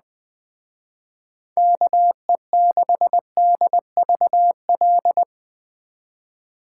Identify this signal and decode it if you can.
Morse code, heard as a keyed tone, about 20 wpm, 700 Hz. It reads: KE6DVL